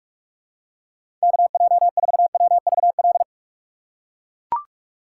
Morse code, heard as a keyed tone, about 44 wpm, 700 Hz. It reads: KJ4WVL
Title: USA Call Signs - 6 Characters - 6x Character Spacing - 45wpm